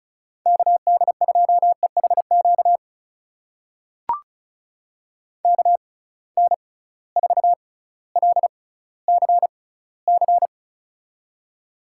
Morse code, heard as a keyed tone, about 36 wpm, spaced as for 18 wpm, 700 Hz. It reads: KD2EHQ KN4LCC
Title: USA Call Signs - 6 Characters - 6x Character Spacing - 35wpm